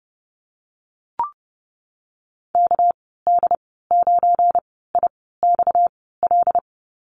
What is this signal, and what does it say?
Morse code, heard as a keyed tone, about 30 wpm, spaced as for 17 wpm, 700 Hz. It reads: KD9IXL
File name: Lesson 37g - Letter x - Call Signs 30wpm